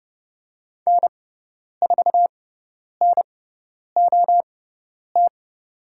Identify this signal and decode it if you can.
Morse code, heard as a keyed tone, about 30 wpm, spaced as for 10 wpm, 700 Hz. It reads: N4NOT
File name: Lesson 18g - Letter m - Call Signs 30wpm